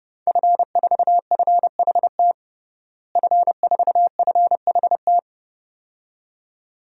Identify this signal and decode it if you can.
Morse code, heard as a keyed tone, about 30 wpm, 700 Hz. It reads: F4FHT F4FHT
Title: DXCC-100 Call Signs - 5 Characters - Repeated 2x - 30wpm